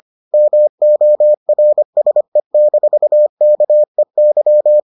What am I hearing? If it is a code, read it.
Morse code, heard as a keyed tone, about 25 wpm, 600 Hz. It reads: MORSE-KEY